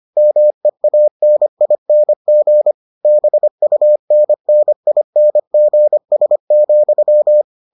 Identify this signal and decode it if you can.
Morse code, heard as a keyed tone, about 25 wpm, 600 Hz. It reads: MEANING BUNNINGS,